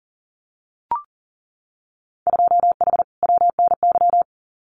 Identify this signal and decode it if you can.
Morse code, heard as a keyed tone, about 40 wpm, 700 Hz. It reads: 2H WNY
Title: Winter Field Day Exchanges Encore 40wpm